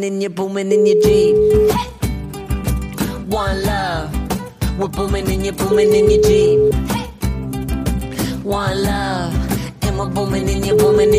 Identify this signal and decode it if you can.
Ringback tone (Europe, 425 Hz), with noise.